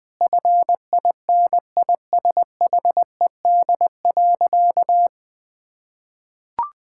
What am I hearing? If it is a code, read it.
Morse code, heard as a keyed tone, about 21 wpm, 700 Hz. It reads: FINISHED.
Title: Sentences with 3-Word Phrases - 20wpm